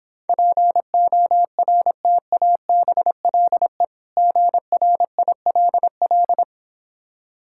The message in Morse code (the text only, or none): PORTABLE GRILL